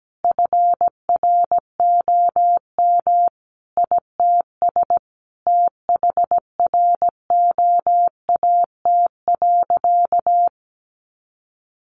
Morse code, heard as a keyed tone, about 17 wpm, 700 Hz. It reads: FROM ITS THROAT.
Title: A Sound Of Thunder 17wpm